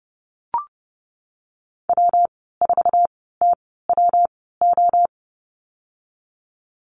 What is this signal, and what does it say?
Morse code, heard as a keyed tone, about 30 wpm, spaced as for 17 wpm, 700 Hz. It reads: W4TWO
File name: Lesson 19g - Letter w - Call Signs 30wpm